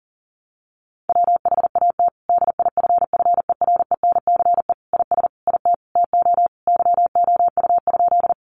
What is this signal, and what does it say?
Morse code, heard as a keyed tone, about 40 wpm, 700 Hz. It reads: WHAT DIFFERENCE IS IT TO YOU?
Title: Sentences from Top 800 Words 40wpm